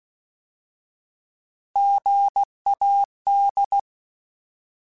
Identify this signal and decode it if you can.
Morse code, heard as a keyed tone, about 16 wpm, 800 Hz. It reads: GAD
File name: Alphabetical / Alphabetical mp3s / Gs @15